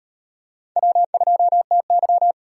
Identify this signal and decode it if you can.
Morse code, heard as a keyed tone, about 38 wpm, 700 Hz. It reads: W2TY